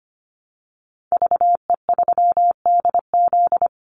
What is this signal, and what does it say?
Morse code, heard as a keyed tone, about 25 wpm, 700 Hz. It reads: VE3DZ
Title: Call Signs - 25wpm